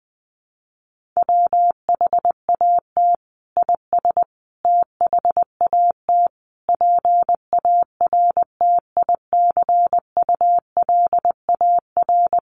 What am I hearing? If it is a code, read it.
Morse code, heard as a keyed tone, about 20 wpm, 700 Hz. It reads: WHAT IS THAT PARTICULAR
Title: Sentences from Top 1500 Words 20wpm